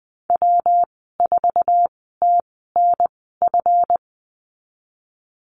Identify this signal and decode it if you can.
Morse code, heard as a keyed tone, about 19 wpm, spaced as for 14 wpm, 700 Hz. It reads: W4TNF
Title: 161 Lesson 26g - Letter g - Call Signs 20wpm